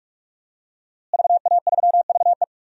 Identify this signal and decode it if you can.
Morse code, heard as a keyed tone, about 46 wpm, 700 Hz. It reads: VA3VE
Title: DXCC-100 Call Signs - 5 Characters - Repeated 3x - 45wpm